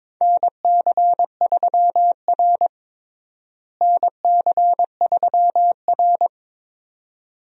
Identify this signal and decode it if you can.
Morse code, heard as a keyed tone, about 22 wpm, 700 Hz. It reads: NC3R NC3R
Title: USA Call Signs - 4 Characters - Repeated 2x - 22wpm